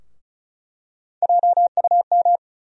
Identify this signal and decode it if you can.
Morse code, heard as a keyed tone, about 34 wpm, 700 Hz. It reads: JUM
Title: USA Call Sign Postfixes - 3 Letters 35wpm